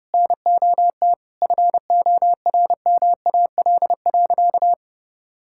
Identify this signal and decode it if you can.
Morse code, heard as a keyed tone, about 30 wpm, 700 Hz. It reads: NOT FORMAL.
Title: Sentences from Top 1600 Words 30wpm